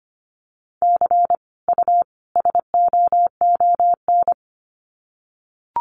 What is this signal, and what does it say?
Morse code, heard as a keyed tone, about 25 wpm, 700 Hz. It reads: C U SOON
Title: Common QSO Phrases 25wpm